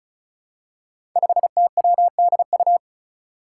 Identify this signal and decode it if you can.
Morse code, heard as a keyed tone, about 36 wpm, 700 Hz. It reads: 5TWDU
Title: Five Challenging Letter-Number Combinations 35wpm